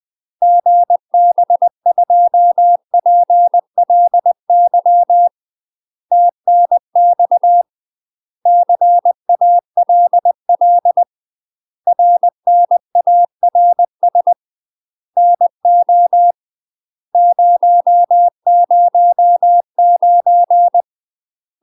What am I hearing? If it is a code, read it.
Morse code, heard as a keyed tone, about 20 wpm, 700 Hz. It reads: GB2PLY TNX CALL RNARS NO 009
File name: UK MASTER QSOs TRAINING / 20WPM / UK MASTER QSOs 0030